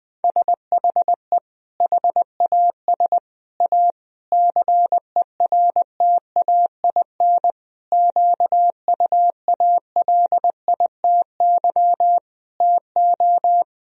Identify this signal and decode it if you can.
Morse code, heard as a keyed tone, about 20 wpm, 700 Hz. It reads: SHE HAS A CERTAIN QUALITY TO
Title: Sentences from Top 1000 Words - Review Part 4 - 20wpm